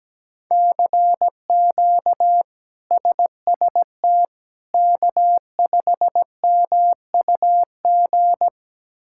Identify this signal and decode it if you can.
Morse code, heard as a keyed tone, about 17 wpm, 700 Hz. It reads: CQ SST K5MUG
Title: K1USN SST QSO Practice - 17wpm